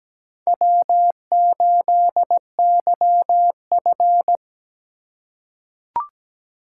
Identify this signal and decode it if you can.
Morse code, heard as a keyed tone, about 17 wpm, 700 Hz. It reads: W8YF
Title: USA Call Signs - 4 Characters - 4x Character Spacing - 17wpm